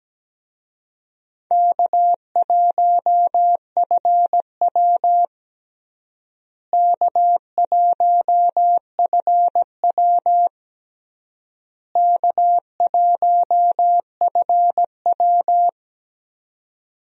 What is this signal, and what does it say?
Morse code, heard as a keyed tone, about 17 wpm, 700 Hz. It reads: K1FW K1FW K1FW
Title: USA Call Signs - 4 Characters - Repeated 3x - 17wpm